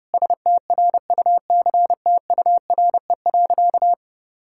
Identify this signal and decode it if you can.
Morse code, heard as a keyed tone, about 30 wpm, 700 Hz. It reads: STRUCTURE.
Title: Sentences from Top 1000 Words - Review Part 6 - 30wpm